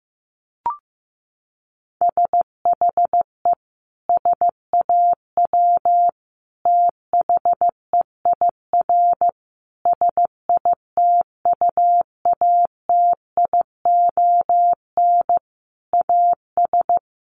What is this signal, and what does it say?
Morse code, heard as a keyed tone, about 14 wpm, 700 Hz. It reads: SHE SAW THEIR SITUATION AS
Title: Sentences from Top 1200 Words 15wpm